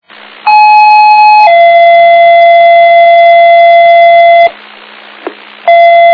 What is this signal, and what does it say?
Two-tone page, heard as a one-second tone, then a three-second tone, about 850 then 680 Hz. It is heard from 0.5 to 4.5 s.
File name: Old Forge Group